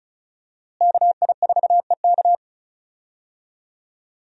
Morse code, heard as a keyed tone, about 36 wpm, 700 Hz. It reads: KI4EK
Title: USA Call Signs - 5 Characters - Repeated 3x - 35wpm